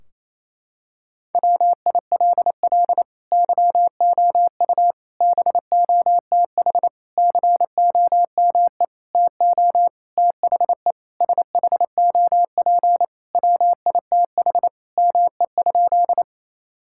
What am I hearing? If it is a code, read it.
Morse code, heard as a keyed tone, about 28 wpm, 700 Hz. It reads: WILL YOU BOTH COME TO THE SHOP WITH ME?
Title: Sentences from Top 1600 Words 28wpm